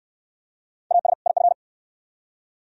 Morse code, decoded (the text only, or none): AS IF